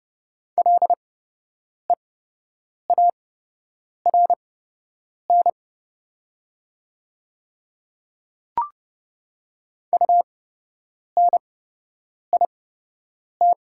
Morse code, heard as a keyed tone, about 29 wpm, spaced as for 8 wpm, 700 Hz. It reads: LEARN UNIT
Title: 8x Character Spacing - Top 500 Words - 30wpm